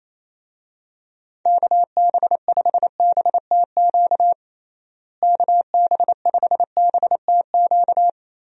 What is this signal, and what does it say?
Morse code, heard as a keyed tone, about 28 wpm, 700 Hz. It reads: KB5BTQ KB5BTQ
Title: USA Call Signs - 6 Characters - Repeated 3x - 28wpm